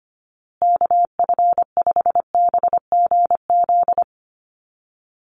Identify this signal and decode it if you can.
Morse code, heard as a keyed tone, about 25 wpm, 700 Hz. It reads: KF5BGZ
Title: USA Call Signs Repeated 3x 25wpm